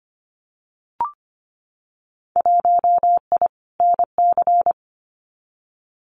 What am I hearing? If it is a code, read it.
Morse code, heard as a keyed tone, about 25 wpm, 690 Hz. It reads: 1I NC
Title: Winter Field Day Exchanges Encore 25wpm